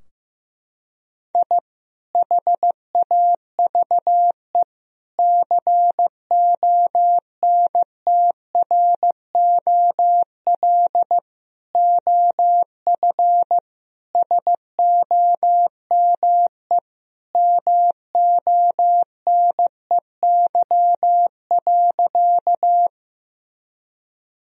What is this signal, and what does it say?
Morse code, heard as a keyed tone, about 15 wpm, 700 Hz. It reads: I HAVE CONTROL OF SOME MONEY.
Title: Sentences from Top 1000 Words - Review Part 3 - 15wpm